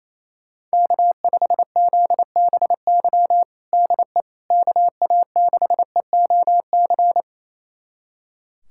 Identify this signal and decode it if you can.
Morse code, heard as a keyed tone, about 28 wpm, 700 Hz. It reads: K5ZBY DE KA6EOC
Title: USA Callsign DE Callsign 28wpm